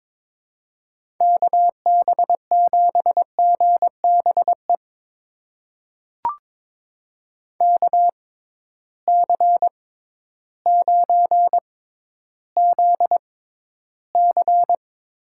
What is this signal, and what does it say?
Morse code, heard as a keyed tone, about 22 wpm, spaced as for 12 wpm, 700 Hz. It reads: KB7GBE KC9ZC
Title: USA Call Signs - 6 Characters - 6x Character Spacing - 22wpm